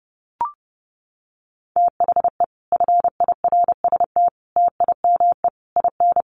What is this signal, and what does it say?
Morse code, heard as a keyed tone, about 30 wpm, 700 Hz. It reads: THE FIRST TIME IN
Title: Sets of 4 Words 30wpm